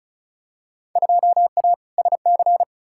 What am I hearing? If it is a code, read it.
Morse code, heard as a keyed tone, about 36 wpm, 700 Hz. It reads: 2A SC
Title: Field Day Exchanges - 35wpm